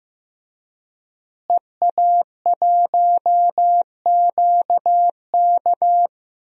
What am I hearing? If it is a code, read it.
Morse code, heard as a keyed tone, about 15 wpm, 700 Hz. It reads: EA1QK
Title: DXCC-100 Call Signs - 5 Characters - Repeated 3x - 15wpm